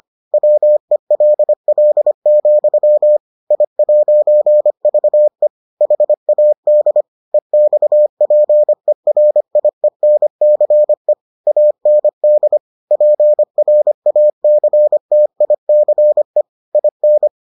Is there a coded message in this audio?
Yes: Morse code from 0.3 to 17.3 s, heard as a keyed tone, about 25 wpm, 600 Hz. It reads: WELL, I'VE HAD EXPERIENCE AND PRACTICE IN